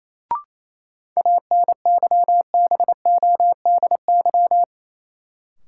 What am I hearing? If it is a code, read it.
Morse code, heard as a keyed tone, about 28 wpm, 700 Hz. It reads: ANYBODY
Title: Sets of 5 Words - New Word Drill - 28wpm